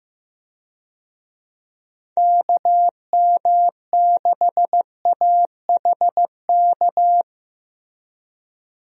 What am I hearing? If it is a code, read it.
Morse code, heard as a keyed tone, about 15 wpm, 700 Hz. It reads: KM6AHK
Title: USA Call Signs - 6 Characters - Repeated 3x - 15wpm